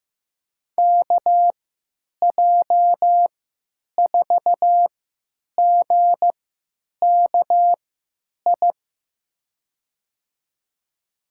Morse code, heard as a keyed tone, about 15 wpm, spaced as for 9 wpm, 700 Hz. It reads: KJ4GKI
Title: USA Call Signs - 6 Characters - 3x Character Spacing - 15wpm